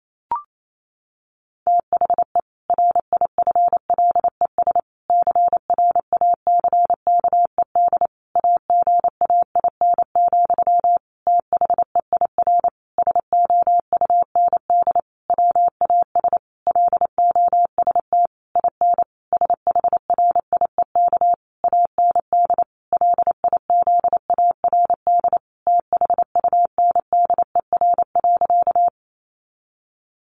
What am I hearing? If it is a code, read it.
Morse code, heard as a keyed tone, about 28 wpm, 700 Hz. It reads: THE RIFLES CRACKED AGAIN, THEIR SOUND WAS LOST IN SHRIEK AND LIZARD THUNDER.
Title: A Sound Of Thunder 28wpm